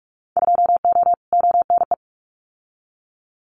Morse code, heard as a keyed tone, about 46 wpm, 700 Hz. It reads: 2O ONE
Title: Winter Field Day Exchanges Encore 45wpm